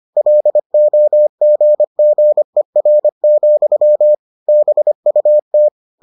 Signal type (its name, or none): Morse code